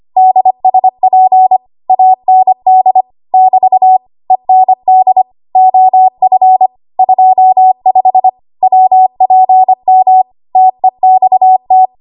Morse code, heard as a keyed tone, about 25 wpm, 750 Hz. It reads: DSP AND <BT> END OF 25 WPM TEXT
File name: Archive / 25 WPM / 25WPM